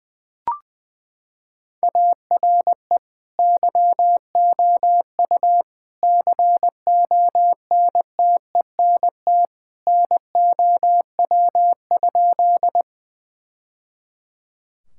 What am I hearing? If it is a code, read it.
Morse code, heard as a keyed tone, about 20 wpm, 700 Hz. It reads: ARE YOU CONTENT NOW?
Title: Sentences from Top 1000 Words - Review Part 6 - 20wpm